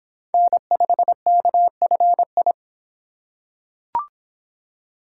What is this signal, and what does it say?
Morse code, heard as a keyed tone, about 25 wpm, 700 Hz. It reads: N5KFI